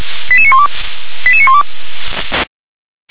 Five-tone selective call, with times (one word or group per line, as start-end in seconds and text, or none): 0.3-0.7 88912
1.3-1.6 88912